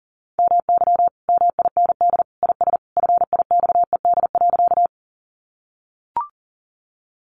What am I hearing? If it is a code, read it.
Morse code, heard as a keyed tone, about 37 wpm, 700 Hz. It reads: MY MIND IS FIXED.
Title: Sentences from Top 1100 Words 40wpm